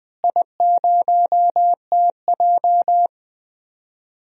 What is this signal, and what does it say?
Morse code, heard as a keyed tone, about 20 wpm, 700 Hz. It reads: I0TJ
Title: DXCC-100 Call Signs - 4 Characters - Repeated 3x - 20wpm